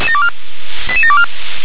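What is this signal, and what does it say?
Five-tone selective call (ZVEI), heard as quick runs of five tones, tones of about 70 ms. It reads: 88724
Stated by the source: Five-tone ZVEI selective call received by SDR.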